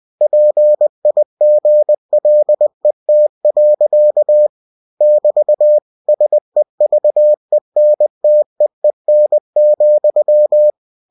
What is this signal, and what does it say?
Morse code, heard as a keyed tone, about 20 wpm, 600 Hz. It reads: PIGLET. <BT> SEVENTEEN,
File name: cw-books / winnie-the-pooh / winnie-20wpm-page-0060